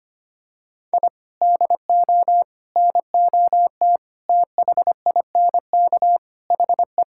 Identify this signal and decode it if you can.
Morse code, heard as a keyed tone, about 25 wpm, 700 Hz. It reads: I DO NOT THINK HE
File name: practice / 25wpm / Sentences with 4-Word Phrases - 25wpm